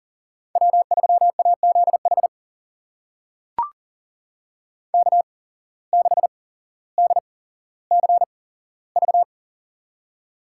Morse code, heard as a keyed tone, about 40 wpm, spaced as for 17 wpm, 700 Hz. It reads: W3AZH K6DCV
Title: USA Call Signs - 5 Characters - 8x Character Spacing - 40wpm